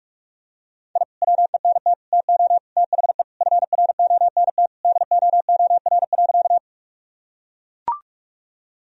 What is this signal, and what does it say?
Morse code, heard as a keyed tone, about 43 wpm, 700 Hz. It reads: I WENT TO THE FRONT DOOR.
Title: Sentences from Top 1000 Words - Review Part 2 - 45wpm